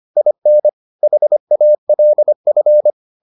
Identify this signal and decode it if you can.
Morse code, heard as a keyed tone, about 25 wpm, 600 Hz. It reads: IN HALF